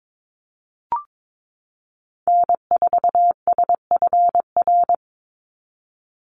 Morse code, heard as a keyed tone, about 22 wpm, 700 Hz. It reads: N4SFR
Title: NA SOTA Activator Callsigns 22wpm